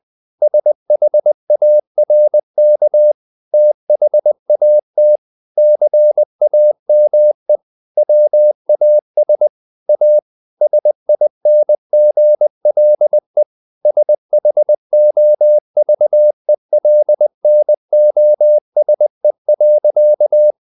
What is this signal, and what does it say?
Morse code, heard as a keyed tone, about 20 wpm, 600 Hz. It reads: SHARK THAT CAME WAS A SINGLE SHOVELNOSE.
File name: the old man and the sea part 2 20wpm